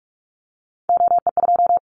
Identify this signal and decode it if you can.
Morse code, heard as a keyed tone, about 44 wpm, 700 Hz. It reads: OE2
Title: Callsign Prefixes for Top 25 DXCC - 45wpm